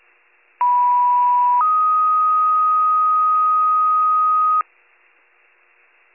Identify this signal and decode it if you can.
Two-tone page, heard as a one-second tone, then a three-second tone, about 980 then 1250 Hz. It is heard from 0.6 to 4.6 s.